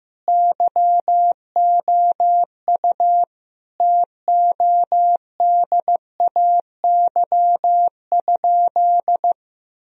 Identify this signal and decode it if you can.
Morse code, heard as a keyed tone, about 15 wpm, 700 Hz. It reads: YOU TODAY?
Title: Sentences from Top 1000 Words - Review Part 3 - 15wpm